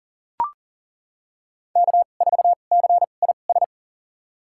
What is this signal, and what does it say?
Morse code, heard as a keyed tone, about 40 wpm, spaced as for 29 wpm, 700 Hz. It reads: K4CIS
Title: USA Call Signs - 5 Characters - 2x Character Spacing - 40wpm